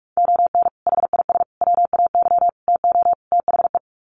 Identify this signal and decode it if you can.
Morse code, heard as a keyed tone, about 44 wpm, 700 Hz. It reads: ON HIS WAY TO THE
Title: Sets of 5 Words Encore2 45wpm